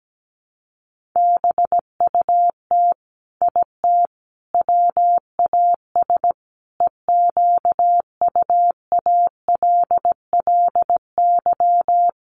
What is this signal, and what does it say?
Morse code, heard as a keyed tone, about 17 wpm, 700 Hz. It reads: BUT IT WAS EQUALLY
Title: Sentences from Top 1400 Words 17wpm